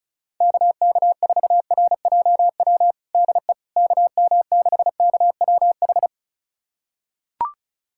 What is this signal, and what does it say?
Morse code, heard as a keyed tone, about 36 wpm, 700 Hz. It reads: KK4RJW DE KM6KWH
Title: USA Callsign DE Callsign 35wpm